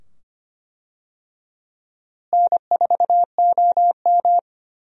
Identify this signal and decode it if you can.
Morse code, heard as a keyed tone, about 25 wpm, 700 Hz. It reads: N4OM
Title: USA Call Signs - 4 Characters - Repeated 2x - 25wpm